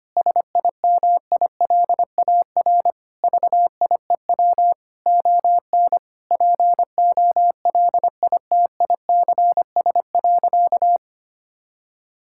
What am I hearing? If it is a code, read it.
Morse code, heard as a keyed tone, about 25 wpm, 700 Hz. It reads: SIMILAR VIEW ON POLITICS.
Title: Sentences from Top 1000 Words - Review Part 6 - 25wpm